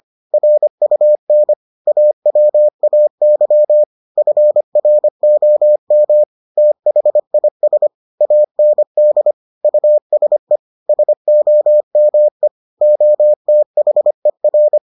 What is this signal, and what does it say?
Morse code, heard as a keyed tone, about 25 wpm, 600 Hz. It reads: RUN AWAY FROM THIS AND USE SOME OTHER